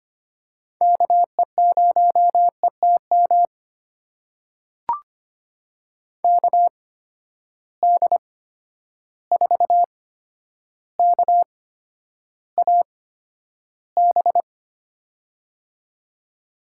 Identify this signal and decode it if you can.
Morse code, heard as a keyed tone, about 25 wpm, spaced as for 11 wpm, 700 Hz. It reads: KE0ETM KD4KAB
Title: USA Call Signs - 6 Characters - 8x Character Spacing - 25wpm